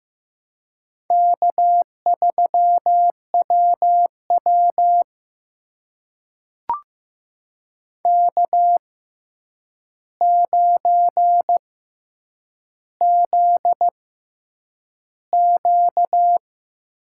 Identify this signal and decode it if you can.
Morse code, heard as a keyed tone, about 15 wpm, 700 Hz. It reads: K 3 W W K 9 Z Q
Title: USA Call Signs - 4 Characters - 6x Character Spacing - 15wpm